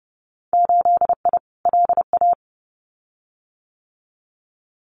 Morse code, heard as a keyed tone, about 30 wpm, 700 Hz. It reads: 8I LA
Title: Winter Field Day Exchanges Encore 30wpm